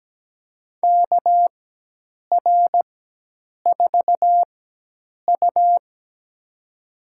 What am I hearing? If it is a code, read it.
Morse code, heard as a keyed tone, about 17 wpm, spaced as for 8 wpm, 700 Hz. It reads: KR4U
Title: USA Call Signs - 4 Characters - 4x Character Spacing - 17wpm